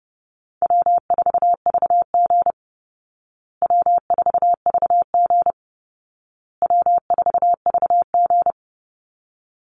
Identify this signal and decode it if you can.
Morse code, heard as a keyed tone, about 30 wpm, 700 Hz. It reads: W4VG W4VG W4VG
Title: Easy Call Signs Repeated 3x - 30wpm